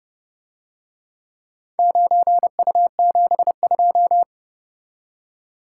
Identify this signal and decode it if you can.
Morse code, heard as a keyed tone, about 30 wpm, 700 Hz. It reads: 9U72